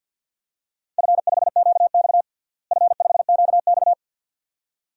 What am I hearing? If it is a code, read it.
Morse code, heard as a keyed tone, about 52 wpm, 700 Hz. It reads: F5QX F5QX
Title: DXCC-100 Call Signs - 4 Characters - Repeated 2x - 50wpm